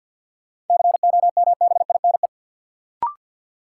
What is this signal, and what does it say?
Morse code, heard as a keyed tone, about 52 wpm, 700 Hz. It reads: COMBINE
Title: Sentences from Top 1600 Words - New Word Drill - 50wpm